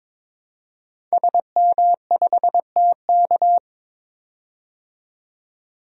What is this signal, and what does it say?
Morse code, heard as a keyed tone, about 22 wpm, 700 Hz. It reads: SM5TK
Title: DXCC-100 Call Signs - 5 Characters - 22wpm